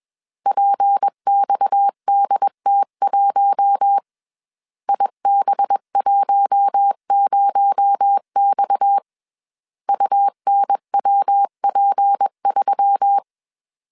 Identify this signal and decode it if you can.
Morse code, heard as a keyed tone, about 21 wpm, 800 Hz. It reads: PXDT1 IB10X UNWP3
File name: CW / DET / 1 6 1500C S A 20wpm 2048